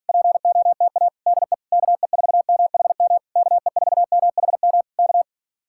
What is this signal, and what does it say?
Morse code, heard as a keyed tone, about 46 wpm, 700 Hz. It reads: POTA DE KE4MHM KE4MHM K